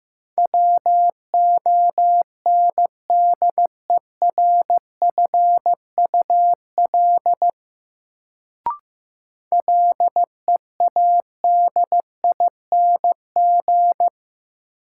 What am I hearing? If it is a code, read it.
Morse code, heard as a keyed tone, about 15 wpm, 700 Hz. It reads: WONDERFUL LEADING
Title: Sentences from Top 1100 Words - New Word Drill - 15wpm